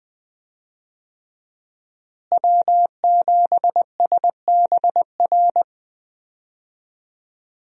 Morse code, heard as a keyed tone, about 20 wpm, 700 Hz. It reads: W7SBR
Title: USA Call Signs - 5 Characters - Repeated 3x - 20wpm